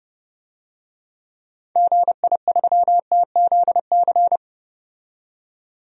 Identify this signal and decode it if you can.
Morse code, heard as a keyed tone, about 30 wpm, 700 Hz. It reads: GI3TZC
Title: DXCC-100 Call Signs - 6 Characters - 30wpm